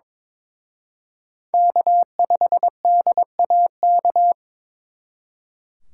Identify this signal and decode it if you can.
Morse code, heard as a keyed tone, about 26 wpm, 700 Hz. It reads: K5DAK
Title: Speed Racing - USA 1x3 Call Signs 22wpm